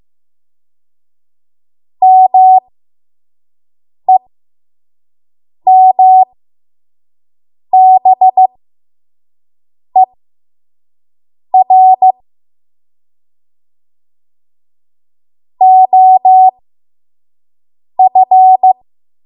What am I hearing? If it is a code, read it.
Morse code, heard as a keyed tone, about 15 wpm, spaced as for 5 wpm, 750 Hz. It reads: MEMBER OF
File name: Morse / 05WPM